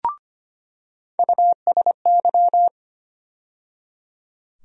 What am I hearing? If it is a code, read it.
Morse code, heard as a keyed tone, about 25 wpm, 700 Hz. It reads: USY
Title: USA Call Sign Postfixes - 3 Letters 25wpm